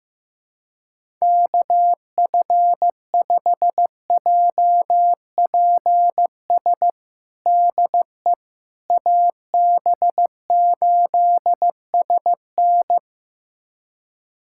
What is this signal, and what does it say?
Morse code, heard as a keyed tone, about 15 wpm, 700 Hz. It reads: KF5JPS DE AB8SN
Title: USA Callsign DE Callsign 15wpm